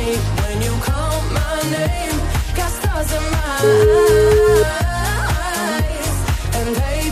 Ringback tone (Europe, 425 Hz), with noise.